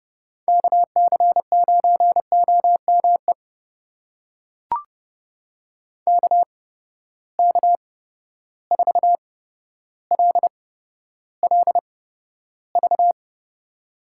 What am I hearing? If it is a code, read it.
Morse code, heard as a keyed tone, about 31 wpm, spaced as for 13 wpm, 700 Hz. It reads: KC9OME KK4LLV
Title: USA Call Signs - 6 Characters - 8x Character Spacing - 30wpm